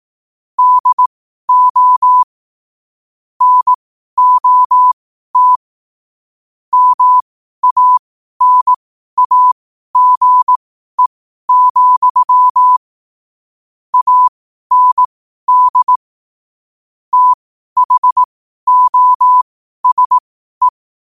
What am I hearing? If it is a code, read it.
Morse code, heard as a keyed tone, about 18 wpm, spaced as for 12 wpm, 1000 Hz. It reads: DO NOT MANAGE, AND THOSE
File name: qotdfiles / 12wpm